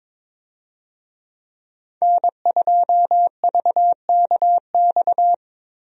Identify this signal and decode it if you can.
Morse code, heard as a keyed tone, about 21 wpm, 700 Hz. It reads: N2VKX
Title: USA Call Signs - 5 Characters - Repeated 2x - 22wpm